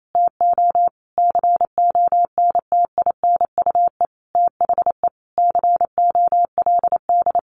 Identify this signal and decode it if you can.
Morse code, heard as a keyed tone, about 28 wpm, 700 Hz. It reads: TO CONTINUE THE COLD
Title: Sentences from Top 1200 Words 28wpm